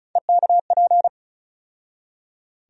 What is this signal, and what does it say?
Morse code, heard as a keyed tone, about 34 wpm, 700 Hz. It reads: EKP
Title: USA Call Sign Postfixes - 3 Letters 35wpm